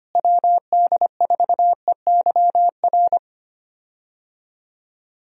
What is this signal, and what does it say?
Morse code, heard as a keyed tone, about 25 wpm, 700 Hz. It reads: WD4EYR
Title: USA Call Signs - 6 Characters - Repeated 3x - 25wpm